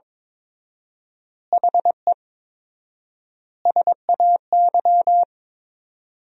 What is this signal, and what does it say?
Morse code, heard as a keyed tone, about 22 wpm, 700 Hz. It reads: HE SAY
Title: 4x Word-Spacing - Sentences from Top 300 Words - 22wpm